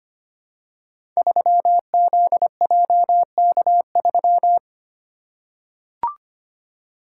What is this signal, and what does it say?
Morse code, heard as a keyed tone, about 25 wpm, 700 Hz. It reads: 3ZJK3
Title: Code Groups - 5 Letter:Numbers - 25wpm